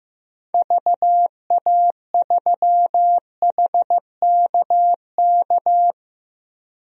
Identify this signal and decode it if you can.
Morse code, heard as a keyed tone, about 15 wpm, 700 Hz. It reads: VA3HKK
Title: DXCC-100 Call Signs - 6 Characters - Repeated 2x - 15wpm